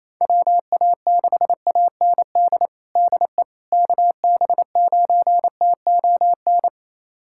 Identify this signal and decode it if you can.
Morse code, heard as a keyed tone, about 27 wpm, 700 Hz. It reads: WA6AND DE KB9TON
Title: USA Callsign DE Callsign 28wpm